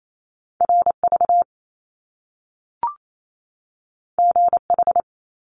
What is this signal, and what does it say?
Morse code, heard as a keyed tone, about 28 wpm, 700 Hz. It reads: RV GH
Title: Top Two Letter Combinations in Words 28wpm